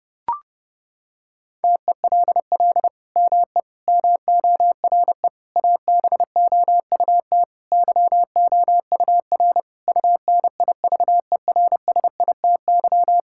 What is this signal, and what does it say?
Morse code, heard as a keyed tone, about 30 wpm, 700 Hz. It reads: TELL ME MORE ABOUT YOUR UNIVERSITY
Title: Sentences from Top 1000 Words - Review Part 5 - 30wpm